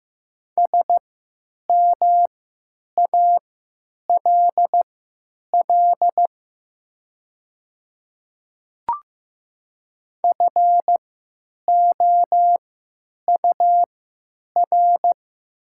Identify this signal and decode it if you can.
Morse code, heard as a keyed tone, about 15 wpm, spaced as for 8 wpm, 700 Hz. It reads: SMALL FOUR
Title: 3x Character Spacing - Top 300 Words - 15wpm